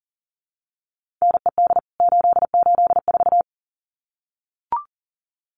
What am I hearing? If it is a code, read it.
Morse code, heard as a keyed tone, about 40 wpm, 700 Hz. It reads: NED 884
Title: CWT Exchange 40wpm